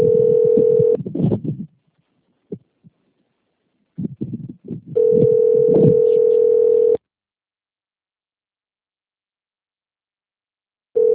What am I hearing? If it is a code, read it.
Ringback tone (North America), 440 + 480 Hz.